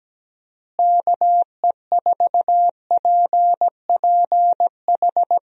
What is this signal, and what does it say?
Morse code, heard as a keyed tone, about 17 wpm, 700 Hz. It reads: KE4PPH
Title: USA Call Signs - 6 Characters - Repeated 3x - 17wpm